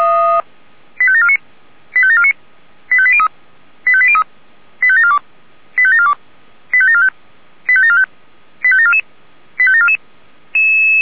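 Five-tone selective call (ZVEI), heard as quick runs of five tones, tones of about 70 ms. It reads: 85749 85749 85803 85803 85742 85742 85745 85745 85744 85744
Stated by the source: Received on SDR, ZVEI five-tone selective call.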